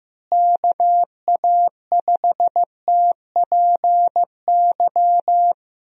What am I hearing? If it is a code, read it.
Morse code, heard as a keyed tone, about 15 wpm, 700 Hz. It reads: KA5TPY
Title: USA Call Signs - 6 Characters - Repeated 3x - 15wpm